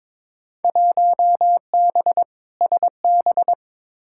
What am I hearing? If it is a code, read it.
Morse code, heard as a keyed tone, about 22 wpm, 700 Hz. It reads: 1B SB
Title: Field Day Exchanges - 22wpm